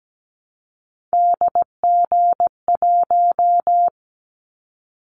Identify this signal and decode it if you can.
Morse code, heard as a keyed tone, about 17 wpm, 700 Hz. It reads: DG1
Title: Callsign Prefixes for Top 25 DXCC - 17wpm